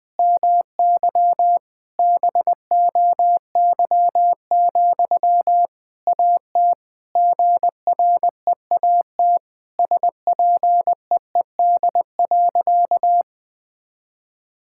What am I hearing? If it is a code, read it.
Morse code, heard as a keyed tone, about 21 wpm, 700 Hz. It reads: MY BOY, AT GREAT SPEED.
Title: Sentences from Top 1000 Words - Review Part 5 - 20wpm